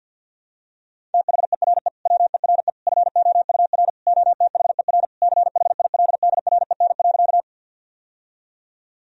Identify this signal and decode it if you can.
Morse code, heard as a keyed tone, about 52 wpm, 700 Hz. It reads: THERE WERE FOUR OTHER CHILDREN.
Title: Sentences from Top 1000 Words - Review Part 1 - 50wpm